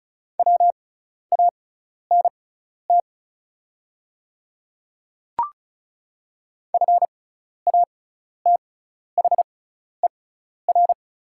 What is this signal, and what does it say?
Morse code, heard as a keyed tone, about 36 wpm, spaced as for 12 wpm, 690 Hz. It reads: WANT FATHER
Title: 6x Character Spacing - Top 200 Words - 35wpm